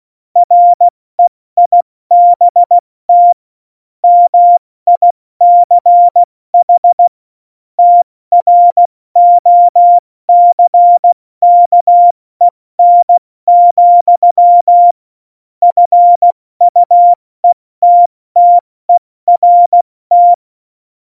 Morse code, heard as a keyed tone, about 16 wpm, 700 Hz. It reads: REIBT MICH TROCKEN, FUETTERT
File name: Morsekurs / Witze / Witz 06. Oktober 2020, 70 F80